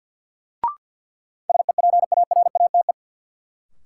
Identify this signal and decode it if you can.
Morse code, heard as a keyed tone, about 46 wpm, 700 Hz. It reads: SEPARATE
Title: Sentences from Top 1000 Words - Review Part 7 - New Word Drill - 50wpm